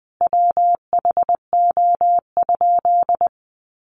Morse code, heard as a keyed tone, about 20 wpm, 700 Hz. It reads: WHO?
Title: Sentences from Top 800 Words 20wpm